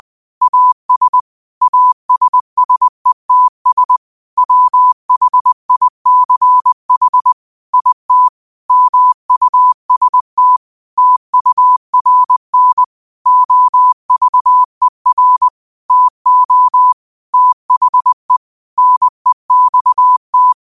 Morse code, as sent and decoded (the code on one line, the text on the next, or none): .- ... / .- ... ... - ... / .-- .... .. -.-. .... / .. - / -- ..- ... - / - ..- .-. -. / --- ...- . .-. / - --- / - .... / -. -..- -
AS ASSETS WHICH IT MUST TURN OVER TO THE NEXT